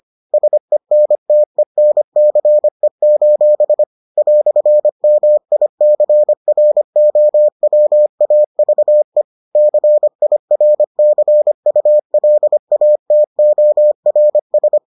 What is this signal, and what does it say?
Morse code, heard as a keyed tone, about 25 wpm, 600 Hz. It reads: SENTENCE: "MICROWAVE CIRCULATORS